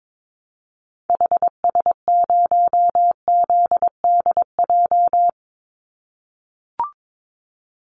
Morse code, heard as a keyed tone, about 22 wpm, 700 Hz. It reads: HS0ZDJ
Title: Call Signs - 22wpm